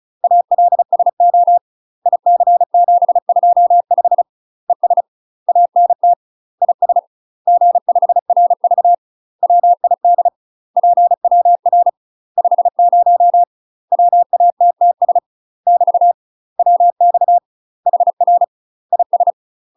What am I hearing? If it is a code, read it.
Morse code, heard as a keyed tone, about 34 wpm, 700 Hz. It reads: ALSO IC725 ES ANT IS G5RV WID PWR 50 WATTS <BT> WX HR IS
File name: UK MASTER QSOs TRAINING / 35WPM / UK MASTER QSOs 0004